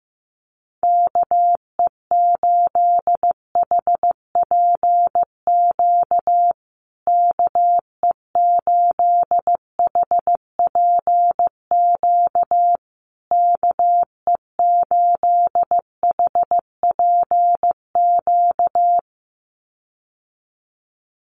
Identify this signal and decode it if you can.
Morse code, heard as a keyed tone, about 14 wpm, 700 Hz. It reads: KE8HPQ KE8HPQ KE8HPQ
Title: USA Call Signs Repeated 3x 15wpm